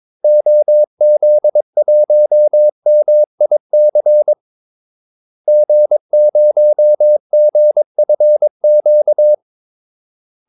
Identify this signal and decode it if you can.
Morse code, heard as a keyed tone, about 22 wpm, 600 Hz. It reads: OZ1MIC G0GFQ
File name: Callsigns-22wpm